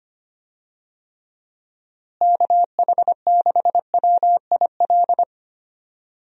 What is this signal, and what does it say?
Morse code, heard as a keyed tone, about 25 wpm, 700 Hz. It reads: KH6WIL
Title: USA Call Signs - 6 Characters - Repeated 3x - 25wpm